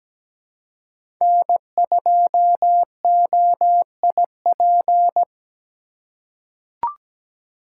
Morse code, heard as a keyed tone, about 17 wpm, 700 Hz. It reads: N2OIP
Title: USA Call Signs - 5 Characters - 3x Character Spacing - 17wpm